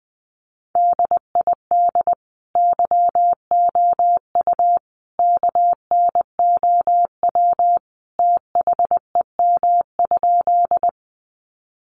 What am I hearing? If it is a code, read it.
Morse code, heard as a keyed tone, about 20 wpm, 700 Hz. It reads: DID YOU KNOW THEM?
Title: Sentences from Top 100 Words 20wpm